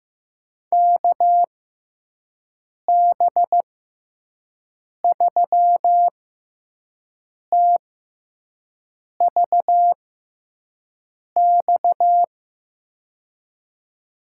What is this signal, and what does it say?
Morse code, heard as a keyed tone, about 15 wpm, spaced as for 5 wpm, 700 Hz. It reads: KB3TVX
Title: USA Call Signs - 6 Characters - 6x Character Spacing - 15wpm